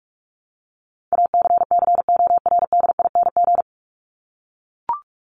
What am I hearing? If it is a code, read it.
Morse code, heard as a keyed tone, about 44 wpm, 700 Hz. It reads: ACCORDING
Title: LY Suffix - Top 100 Words with Base Words 45wpm